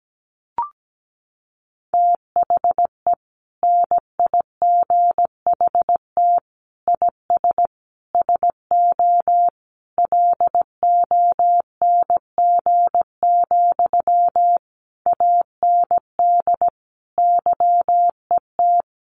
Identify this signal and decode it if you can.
Morse code, heard as a keyed tone, about 17 wpm, 700 Hz. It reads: THE NIGHT IS SO LONG, AND YET
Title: Sentences from Top 500 Words 17wpm